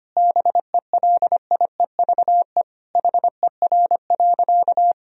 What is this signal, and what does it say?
Morse code, heard as a keyed tone, about 25 wpm, 700 Hz. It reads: BELIEVE HER.
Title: Sentences from Top 1000 Words - Review Part 4 - 25wpm